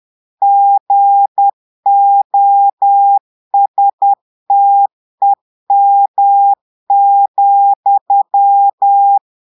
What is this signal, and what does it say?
Morse code, heard as a keyed tone, about 10 wpm, 800 Hz. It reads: GOSTEM,
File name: portugais texte en clair 2 vitesse 10 mots minutes